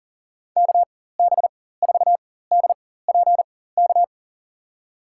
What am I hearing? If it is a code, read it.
Morse code, heard as a keyed tone, about 37 wpm, spaced as for 18 wpm, 700 Hz. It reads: KB4DPK
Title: USA Call Signs - 6 Characters - 4x Character Spacing - 40wpm